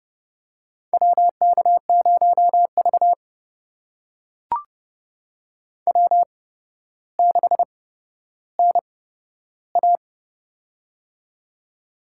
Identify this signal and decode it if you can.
Morse code, heard as a keyed tone, about 31 wpm, spaced as for 13 wpm, 700 Hz. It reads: WK0V W6NA
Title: USA Call Signs - 4 Characters - 8x Character Spacing - 30wpm